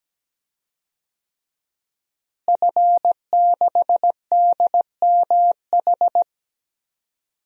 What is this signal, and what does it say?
Morse code, heard as a keyed tone, about 17 wpm, 700 Hz. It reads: F6DMH
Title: DXCC-100 Call Signs - 5 Characters - Repeated 3x - 17wpm